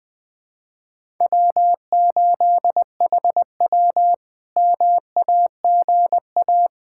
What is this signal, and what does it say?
Morse code, heard as a keyed tone, about 20 wpm, 700 Hz. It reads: W8HW MAGA
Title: K1USN SST QSO Practice - 20wpm